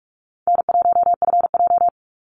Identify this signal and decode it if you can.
Morse code, heard as a keyed tone, about 46 wpm, 700 Hz. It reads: N1FJ
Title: NA SOTA Activator Callsigns 45wpm